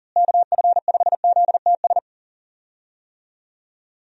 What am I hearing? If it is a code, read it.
Morse code, heard as a keyed tone, about 40 wpm, 700 Hz. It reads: KF5ZTS
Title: USA Call Signs - 6 Characters - Repeated 3x - 40wpm